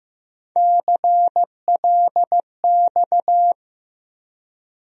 Morse code, heard as a keyed tone, about 15 wpm, 700 Hz. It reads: CLX
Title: USA Call Sign Postfixes - 3 Letters 15wpm